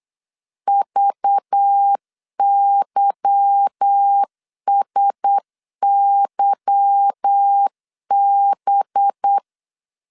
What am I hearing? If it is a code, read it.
Morse code, heard as a keyed tone, about 8 wpm, 800 Hz. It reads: VYSYB
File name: CW / DET / 1 9 1500C S B 8wpm 5624